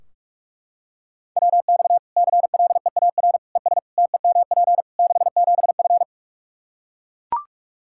Morse code, heard as a keyed tone, about 41 wpm, 700 Hz. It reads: WX CLEAR ES TEMP 67F
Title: K1USN SST QSO Practice 45wpm